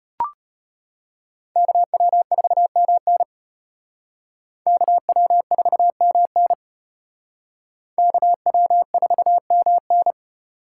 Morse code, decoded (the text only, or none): KW4MN KW4MN KW4MN